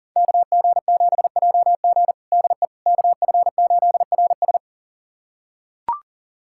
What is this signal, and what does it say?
Morse code, heard as a keyed tone, about 40 wpm, 700 Hz. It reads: KG7JG DE KF8RS
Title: USA Callsign DE Callsign - 5 Characters - 40wpm